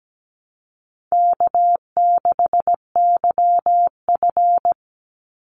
Morse code, heard as a keyed tone, about 17 wpm, 700 Hz. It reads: K6YF
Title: Easy Call Signs - 17wpm